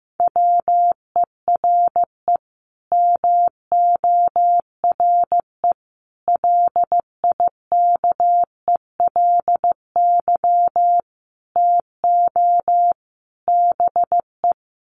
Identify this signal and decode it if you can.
Morse code, heard as a keyed tone, about 15 wpm, 700 Hz. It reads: WERE MORE LIKELY TO BE
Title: Sets of 5 Words Encore2 15wpm